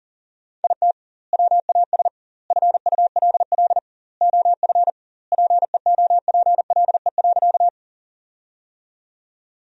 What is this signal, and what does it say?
Morse code, heard as a keyed tone, about 40 wpm, 700 Hz. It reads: IT WAS FULL OF PEOPLE.
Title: 2x Word-Spacing - Sentences from Top 500 Words - 40wpm